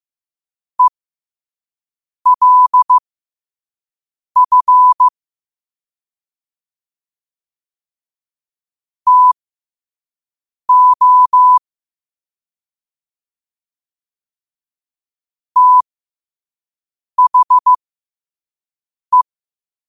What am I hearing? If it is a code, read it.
Morse code, heard as a keyed tone, about 15 wpm, spaced as for 5 wpm, 1000 Hz. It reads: ELF TO THE
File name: qotdfiles / 5wpm